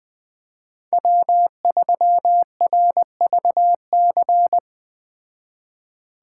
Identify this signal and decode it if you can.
Morse code, heard as a keyed tone, about 20 wpm, 700 Hz. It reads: W3RVC
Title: Speed Racing - USA 1x3 Call Signs 17wpm